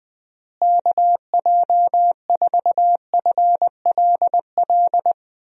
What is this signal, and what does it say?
Morse code, heard as a keyed tone, about 20 wpm, 700 Hz. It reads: KJ4FLL
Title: USA Call Signs - 6 Characters - Repeated 3x - 20wpm